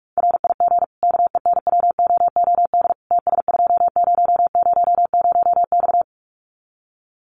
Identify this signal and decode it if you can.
Morse code, heard as a keyed tone, about 44 wpm, 700 Hz. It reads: RIG KENWOOD TS2000X
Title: Common QSO Phrases 45wpm